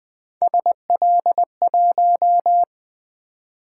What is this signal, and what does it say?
Morse code, heard as a keyed tone, about 20 wpm, 700 Hz. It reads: SL1
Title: Three Letter-Number Combinations - 20wpm